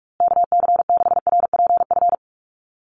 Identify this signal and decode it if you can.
Morse code, heard as a keyed tone, about 46 wpm, 690 Hz. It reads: KC6RPF
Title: USA Call Signs Repeated 3x 45wpm